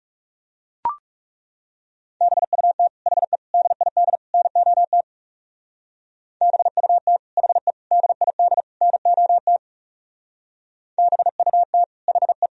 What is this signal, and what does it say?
Morse code, heard as a keyed tone, about 39 wpm, 700 Hz. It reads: BUT HE DID NOT BUT HE DID NOT BUT HE